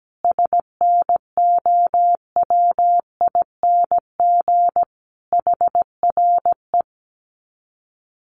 Morse code, heard as a keyed tone, about 17 wpm, 700 Hz. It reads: SNOWING HRE
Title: Common QSO Phrases 17wpm